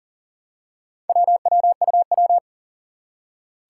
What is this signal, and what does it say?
Morse code, heard as a keyed tone, about 40 wpm, 700 Hz. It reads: WWUW
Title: Four Challenging Letter-Number Combinations 40wpm